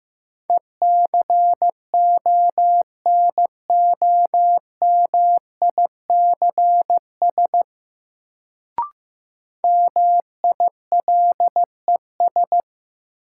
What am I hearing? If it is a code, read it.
Morse code, heard as a keyed tone, about 15 wpm, 700 Hz. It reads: ECONOMICS MILES
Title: Sentences from Top 1000 Words - Review Part 3 - New Word Drill - 15wpm